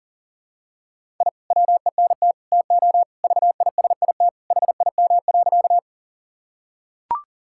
Morse code, heard as a keyed tone, about 40 wpm, 700 Hz. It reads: I WENT TO VISIT HIM.
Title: Sentences from Top 1000 Words - Review Part 6 - 40wpm